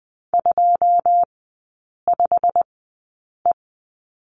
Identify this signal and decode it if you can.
Morse code, heard as a keyed tone, about 20 wpm, spaced as for 8 wpm, 700 Hz. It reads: 25E
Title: 127 Lesson 21e - Number 6 - Three Characters 20wpm